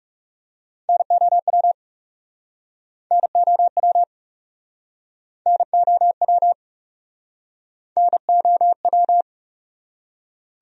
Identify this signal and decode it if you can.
Morse code, heard as a keyed tone, about 40 wpm, 700 Hz. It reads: NOW NOW NOW NOW